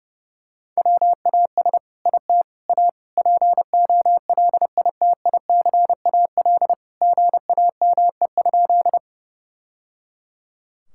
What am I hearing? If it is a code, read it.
Morse code, heard as a keyed tone, about 30 wpm, 700 Hz. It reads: WAS IT A POLITICAL GAME?
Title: Sentences from Top 1000 Words - Review Part 5 - 30wpm